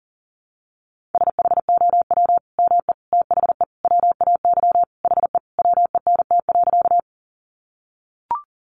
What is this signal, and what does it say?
Morse code, heard as a keyed tone, about 40 wpm, 700 Hz. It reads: SHOW ME THE WAY HE WENT.
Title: Sentences from Top 200 Words 40wpm